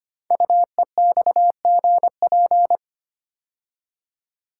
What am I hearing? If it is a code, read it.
Morse code, heard as a keyed tone, about 25 wpm, 700 Hz. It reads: UEXGP
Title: Code Groups - 5 Letters - 25wpm